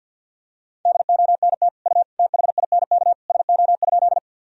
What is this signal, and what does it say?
Morse code, heard as a keyed tone, about 52 wpm, 700 Hz. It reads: DONT U THINK SO?
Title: Sentences from Top 300 Words and Abbreviations 50wpm